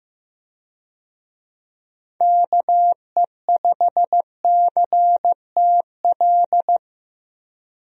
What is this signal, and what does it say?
Morse code, heard as a keyed tone, about 15 wpm, 700 Hz. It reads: KE5CTL
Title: USA Call Signs - 6 Characters - 15wpm